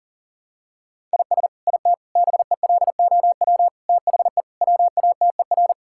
Morse code, heard as a keyed tone, about 40 wpm, 700 Hz. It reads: IS IT BELOW THE WATER
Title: Sentences from Top 1000 Words - Review Part 5 - 40wpm